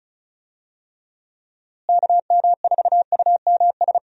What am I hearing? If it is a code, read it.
Morse code, heard as a keyed tone, about 36 wpm, 700 Hz. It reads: KM4UMS
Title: USA Call Signs - 6 Characters - Repeated 2x - 35wpm